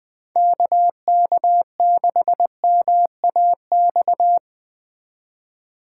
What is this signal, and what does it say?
Morse code, heard as a keyed tone, about 20 wpm, 700 Hz. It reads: KK6MAX
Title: Speed Racing - USA Call Signs 17wpm